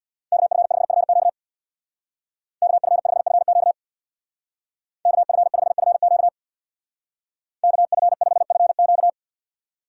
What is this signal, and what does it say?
Morse code, heard as a keyed tone, about 65 wpm, 700 Hz. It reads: KF5FQ KF5FQ KF5FQ KF5FQ